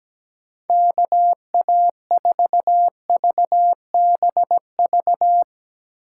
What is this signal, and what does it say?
Morse code, heard as a keyed tone, about 17 wpm, 700 Hz. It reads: KA4VBV
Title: USA Call Signs - 6 Characters - 3x Character Spacing - 17wpm